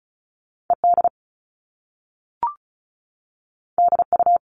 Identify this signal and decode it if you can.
Morse code, heard as a keyed tone, about 36 wpm, 690 Hz. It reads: ED DU
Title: All Cut Numbers - 2-Digit 35wpm